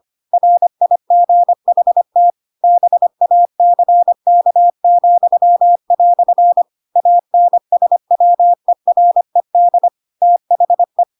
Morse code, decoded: RIGHT BACK," ANSWERED THE

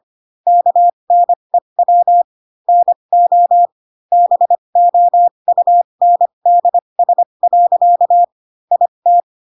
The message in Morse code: KNEW NO BOUNDS. IT

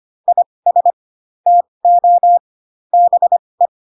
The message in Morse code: IS TO BE

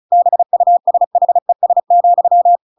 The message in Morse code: BUSHES,

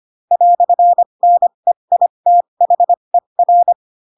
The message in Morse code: "NEITHER

